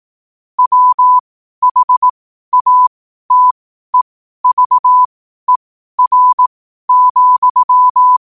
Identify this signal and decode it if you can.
Morse code, heard as a keyed tone, about 18 wpm, spaced as for 12 wpm, 1000 Hz. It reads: WHATEVER,
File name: qotdfiles / 12wpm